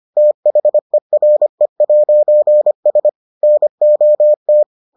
Morse code, heard as a keyed tone, about 25 wpm, 600 Hz. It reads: THERE'S NOT